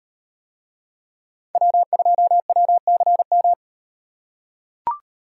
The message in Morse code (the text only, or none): W2WCM